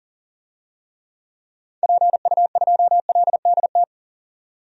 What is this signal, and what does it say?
Morse code, heard as a keyed tone, about 40 wpm, 700 Hz. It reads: PU2LDT
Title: DXCC-100 Call Signs - 6 Characters - 40wpm